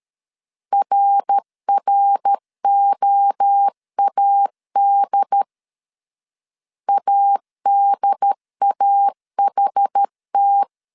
Morse code, heard as a keyed tone, about 13 wpm, 800 Hz. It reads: RROAD ADAHT
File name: CW / DET / 1 3 1500C S C 12wpm 2750